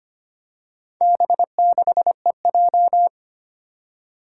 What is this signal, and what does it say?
Morse code, heard as a keyed tone, about 25 wpm, 700 Hz. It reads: B6EJ
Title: Four Challenging Letter-Number Combinations 25wpm